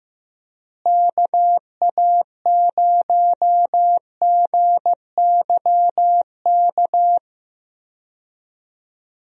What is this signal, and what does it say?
Morse code, heard as a keyed tone, about 15 wpm, 700 Hz. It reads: KA0GYK
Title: USA Call Signs - 6 Characters - Repeated 2x - 15wpm